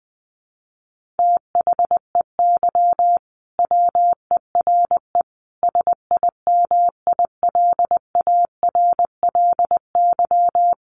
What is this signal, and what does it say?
Morse code, heard as a keyed tone, about 20 wpm, 700 Hz. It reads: THEY WERE SIMILARLY
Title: Sentences from Top 1400 Words 20wpm